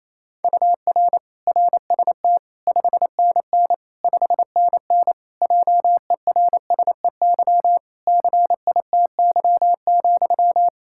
Morse code, heard as a keyed tone, about 28 wpm, 700 Hz. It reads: UR RST 5NN 5NN JERSEY CITY,
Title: Practice QSO - Exchange 1 - Beginning - 28wpm